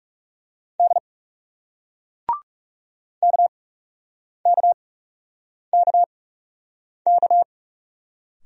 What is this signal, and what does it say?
Morse code, heard as a keyed tone, about 38 wpm, 700 Hz. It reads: D K K K K